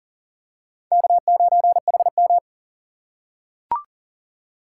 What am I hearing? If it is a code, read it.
Morse code, heard as a keyed tone, about 40 wpm, 700 Hz. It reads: K9HM
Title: USA Call Signs - 4 Characters - 2x Character Spacing - 40wpm